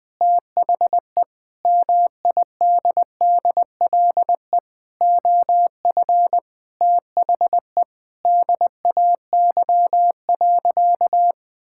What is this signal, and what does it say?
Morse code, heard as a keyed tone, about 20 wpm, 700 Hz. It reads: THE MIDDLE OF THE DAY.
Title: Sentences from Top 1000 Words - Review Part 6 - 20wpm